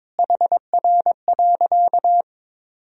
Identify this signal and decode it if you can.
Morse code, heard as a keyed tone, about 22 wpm, 700 Hz. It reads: HR.
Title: Sentences from Top 500 Words and Abbreviations 22wpm